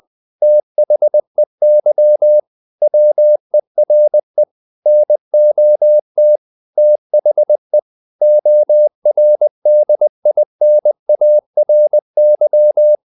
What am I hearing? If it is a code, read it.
Morse code, heard as a keyed tone, about 20 wpm, 600 Hz. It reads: THEY WERE NOT THE ORDINARY
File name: the old man and the sea part 2 20wpm